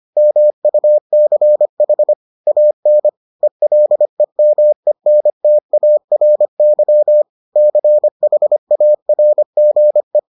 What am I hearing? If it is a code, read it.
Morse code, heard as a keyed tone, about 25 wpm, 600 Hz. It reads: MUCH AN ELEMENTARY CHARGE